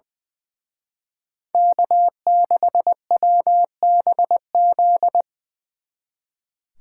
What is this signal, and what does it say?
Morse code, heard as a keyed tone, about 22 wpm, 700 Hz. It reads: K6WBZ
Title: Speed Racing - USA Call Signs 20wpm